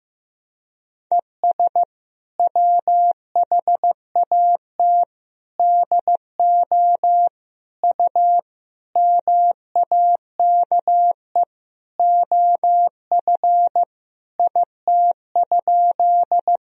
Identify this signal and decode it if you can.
Morse code, heard as a keyed tone, about 15 wpm, 700 Hz. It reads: ES WHAT DO U MAKE OF IT?
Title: Sentences from Top 100 Words and Abbreviations 15wpm